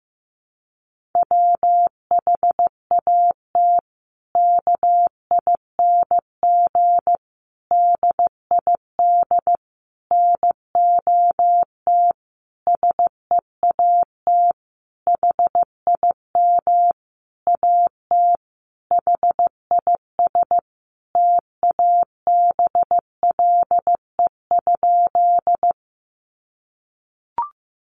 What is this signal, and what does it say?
Morse code, heard as a keyed tone, about 15 wpm, 700 Hz. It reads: WHAT KING DID NOT SEAT HIM AT HIS TABLE?
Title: Sentences from Top 1500 Words 15wpm